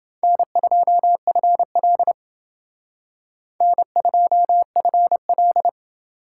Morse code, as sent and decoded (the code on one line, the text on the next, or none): -. ..--- ..-. .-.. / -. ..--- ..-. .-..
N2FL N2FL